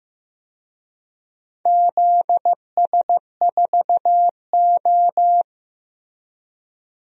Morse code, heard as a keyed tone, about 16 wpm, 700 Hz. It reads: ZS4O
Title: DXCC-100 Call Signs - 4 Characters - Repeated 3x - 15wpm